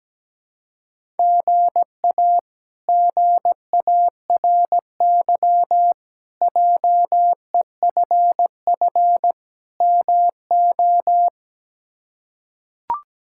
Morse code, heard as a keyed tone, about 17 wpm, 700 Hz. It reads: GA GARY JEFF MO
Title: K1USN SST QSO Practice - 17wpm